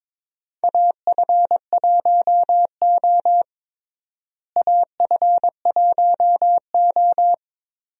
Morse code, heard as a keyed tone, about 22 wpm, 700 Hz. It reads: AF1O AF1O
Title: USA Call Signs - 4 Characters - Repeated 2x - 22wpm